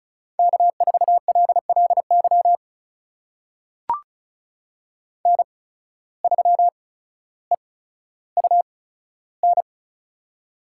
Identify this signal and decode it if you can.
Morse code, heard as a keyed tone, about 34 wpm, spaced as for 15 wpm, 700 Hz. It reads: K4LLY N3EUN
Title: USA Call Signs - 5 Characters - 8x Character Spacing - 35wpm